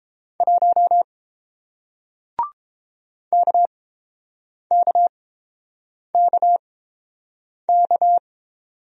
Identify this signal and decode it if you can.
Morse code, heard as a keyed tone, about 30 wpm, spaced as for 7 wpm, 700 Hz. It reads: 1 KKKK